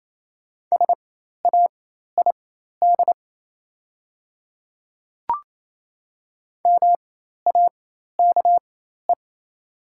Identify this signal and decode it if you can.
Morse code, heard as a keyed tone, about 28 wpm, spaced as for 13 wpm, 690 Hz. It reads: SAID MAKE
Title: 4x Character Spacing - Top 100 Words - 28wpm